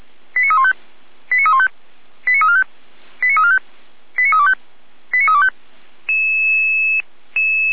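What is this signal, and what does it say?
Five-tone selective call (ZVEI), heard as quick runs of five tones, tones of about 70 ms. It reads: 89315 89315 89345 89345 89325 89325
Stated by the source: ZVEI five-tone selective call 89340 received on SDR.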